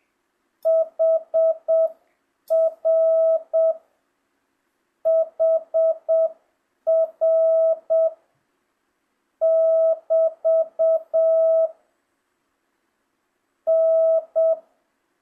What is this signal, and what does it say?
Morse code, heard as a keyed tone, about 7 wpm, 1300 Hz, with noise. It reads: HR HR <BT> N
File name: jh8cbh / 2025cw